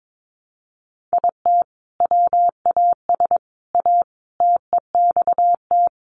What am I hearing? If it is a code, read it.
Morse code, heard as a keyed tone, about 22 wpm, 700 Hz. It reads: IT WAS A TEXT
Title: Sentences from Top 1200 Words 22wpm